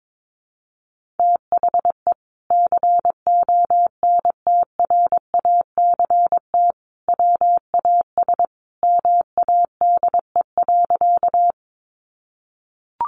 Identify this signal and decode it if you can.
Morse code, heard as a keyed tone, about 22 wpm, 700 Hz. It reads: THE CONTRACT WAS MADE.
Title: Sentences from Top 1400 Words 22wpm